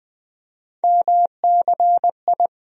Morse code, heard as a keyed tone, about 20 wpm, 700 Hz. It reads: MCI
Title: USA Call Sign Postfixes - 3 Letters 20wpm